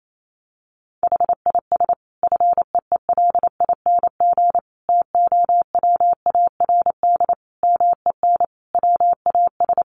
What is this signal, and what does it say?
Morse code, heard as a keyed tone, about 28 wpm, 700 Hz. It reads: HIS FEELING TOWARD MEN WAS
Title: Sentences from Top 1300 Words 28wpm